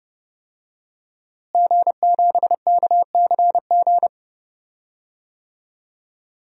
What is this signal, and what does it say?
Morse code, heard as a keyed tone, about 30 wpm, 700 Hz. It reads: G7KCG
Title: DXCC-100 Call Signs - 5 Characters - Repeated 3x - 30wpm